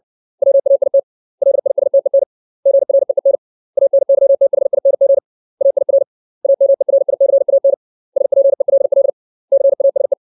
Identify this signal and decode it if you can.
Morse code, heard as a keyed tone, about 60 wpm, 550 Hz. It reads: WAIT LISTEN GREEN ANYTHING AIR AMERICAN SPELL CASE
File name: 632 words / 632 Words 60WPM